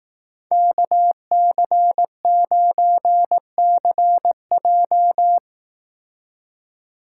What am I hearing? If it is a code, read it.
Morse code, heard as a keyed tone, about 18 wpm, 700 Hz. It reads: KC9CJ
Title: Speed Racing - USA 2x2 Call Signs 15wpm